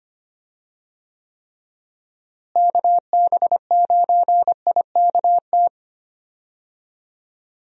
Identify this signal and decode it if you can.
Morse code, heard as a keyed tone, about 25 wpm, 700 Hz. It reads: KB9IKT
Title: USA Call Signs - 6 Characters - Repeated 3x - 25wpm